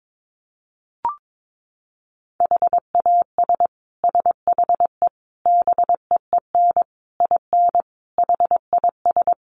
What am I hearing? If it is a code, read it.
Morse code, heard as a keyed tone, about 22 wpm, 700 Hz. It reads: HAS SHE BEEN IN HIS
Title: Sentences from Top 700 Words 22wpm